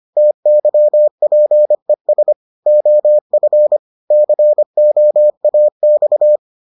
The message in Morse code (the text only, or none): TYPES OF COAX